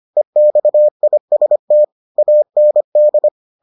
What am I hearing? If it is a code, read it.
Morse code, heard as a keyed tone, about 25 wpm, 600 Hz. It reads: EXIST AND